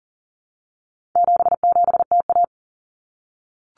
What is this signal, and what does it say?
Morse code, heard as a keyed tone, about 40 wpm, 700 Hz. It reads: 77TA
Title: Common Cut Numbers - 4-Digit 40wpm